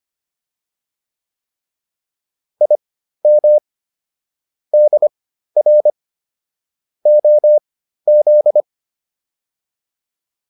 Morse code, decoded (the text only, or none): IM DR OZ